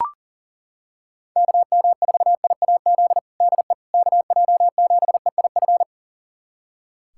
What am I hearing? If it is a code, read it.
Morse code, heard as a keyed tone, about 37 wpm, 700 Hz. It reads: KM4IAZ DE KJ7EIF
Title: USA Callsign DE Callsign 40wpm